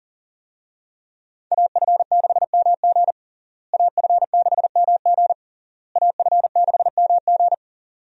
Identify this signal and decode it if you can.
Morse code, heard as a keyed tone, about 40 wpm, 700 Hz. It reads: AF6MG AF6MG AF6MG
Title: USA Call Signs - 5 Characters - Repeated 3x - 40wpm